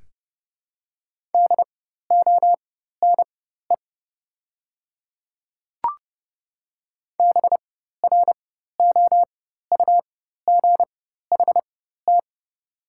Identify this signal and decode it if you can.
Morse code, heard as a keyed tone, about 30 wpm, spaced as for 14 wpm, 700 Hz. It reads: DONE BROUGHT
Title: 4x Character Spacing - Top 500 Words - 30wpm